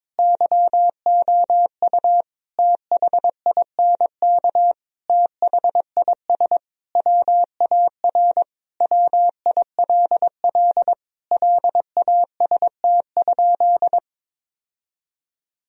Morse code, heard as a keyed tone, about 22 wpm, 700 Hz. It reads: YOU THINK THIS WAR WILL LAST?
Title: Sentences from Top 1000 Words - Review Part 2 - 22wpm